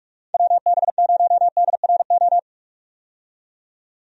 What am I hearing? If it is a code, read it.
Morse code, heard as a keyed tone, about 46 wpm, 700 Hz. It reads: WB0DRO
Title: USA Call Signs - 6 Characters - 45wpm